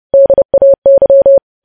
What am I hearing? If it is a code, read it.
Morse code, heard as a keyed tone, about 30 wpm, 550 Hz. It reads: DAY
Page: News Headlines in Morse Code at 30 WPM